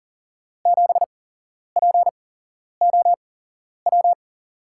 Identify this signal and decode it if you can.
Morse code, heard as a keyed tone, about 38 wpm, spaced as for 11 wpm, 700 Hz. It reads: 7POW
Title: USA Call Signs - 8x Character Spacing - 40wpm